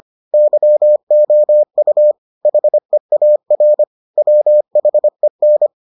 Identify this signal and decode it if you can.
Morse code, heard as a keyed tone, about 25 wpm, 600 Hz. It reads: YOU HEAR WHEN